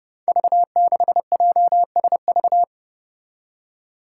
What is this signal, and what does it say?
Morse code, heard as a keyed tone, about 30 wpm, 700 Hz. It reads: V6JSV
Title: Five Challenging Letter-Number Combinations 30wpm